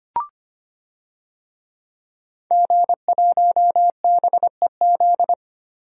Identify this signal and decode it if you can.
Morse code, heard as a keyed tone, about 25 wpm, 700 Hz. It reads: G1BEZ
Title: DXCC-100 Call Signs - 5 Characters - Repeated 2x - 25wpm